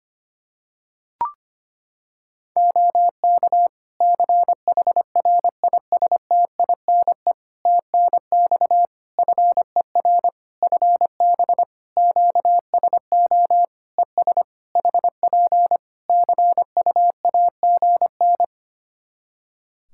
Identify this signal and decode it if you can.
Morse code, heard as a keyed tone, about 25 wpm, 700 Hz. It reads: OK CHRISTINE TNX FER FB QSO ES HP CUAGN
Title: Practice QSO - Exchange 4 - Ending - 25wpm